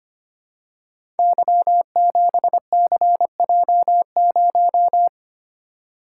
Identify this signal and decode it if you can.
Morse code, heard as a keyed tone, about 25 wpm, 700 Hz. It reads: Y7CJ0
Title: Code Groups - 5 Letter:Numbers - 25wpm